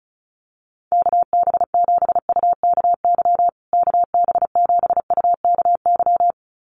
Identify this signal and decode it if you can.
Morse code, heard as a keyed tone, about 34 wpm, 700 Hz. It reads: KB7UKY KB7UKY
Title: USA Call Signs Repeated 2x 35wpm